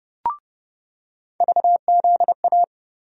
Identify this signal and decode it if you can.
Morse code, heard as a keyed tone, about 30 wpm, 700 Hz. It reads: VZA
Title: USA Call Sign Postfixes - 3 Letters 30wpm